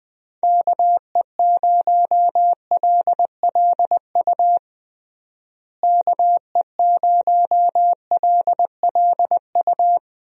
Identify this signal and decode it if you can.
Morse code, heard as a keyed tone, about 21 wpm, 700 Hz. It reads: KE0LLU KE0LLU
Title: USA Call Signs - 6 Characters - Repeated 3x - 20wpm